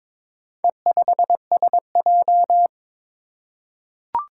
Morse code, heard as a keyed tone, about 22 wpm, 700 Hz. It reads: E5SJ
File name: practice / 22wpm / Four Letter-Number Combinations - 22wpm